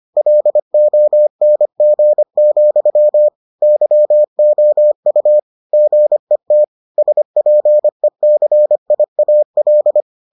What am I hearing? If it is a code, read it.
Morse code, heard as a keyed tone, about 25 wpm, 600 Hz. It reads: LONG, YOU GET SPECIAL